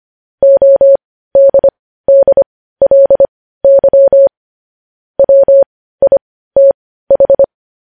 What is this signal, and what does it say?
Morse code, heard as a keyed tone, about 25 wpm, spaced as for 15 wpm, 550 Hz. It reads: ODDLY WITH
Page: News headlines at 15